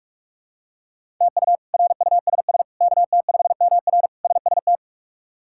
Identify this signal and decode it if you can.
Morse code, heard as a keyed tone, about 46 wpm, 700 Hz. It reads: TU RUSS KT5MR SST
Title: K1USN SST QSO Practice - 45wpm